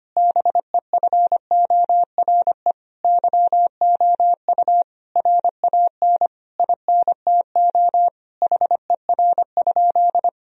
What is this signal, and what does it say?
Morse code, heard as a keyed tone, about 25 wpm, 700 Hz. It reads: BEFORE YOU RAN INTO HER?
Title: Sentences from Top 1000 Words - Review Part 3 - 25wpm